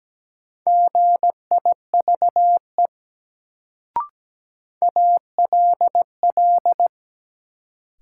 Morse code, heard as a keyed tone, about 17 wpm, 700 Hz. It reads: GIVE ALL
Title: Sets of 2 Words - New Word Drill - 17wpm